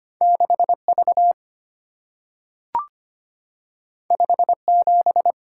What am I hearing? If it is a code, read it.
Morse code, heard as a keyed tone, about 25 wpm, 700 Hz. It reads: 6V 57
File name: Two Letter-Number Combinations - 25wpm